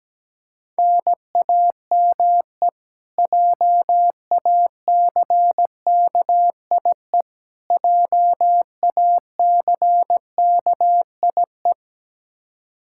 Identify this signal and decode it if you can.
Morse code, heard as a keyed tone, about 17 wpm, 700 Hz. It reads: NAME JACKIE JACKIE
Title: Practice QSO - Exchange 1 - Beginning - 17wpm